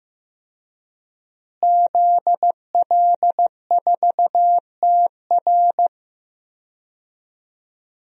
Morse code, heard as a keyed tone, about 15 wpm, 700 Hz. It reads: ZL4TR
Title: DXCC-100 Call Signs - 5 Characters - Repeated 2x - 15wpm